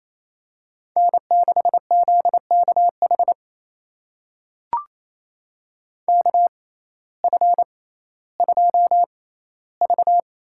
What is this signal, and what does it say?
Morse code, heard as a keyed tone, about 28 wpm, spaced as for 15 wpm, 700 Hz. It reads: N6ZKH KF2V
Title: USA Call Signs - 5 Characters - 6x Character Spacing - 28wpm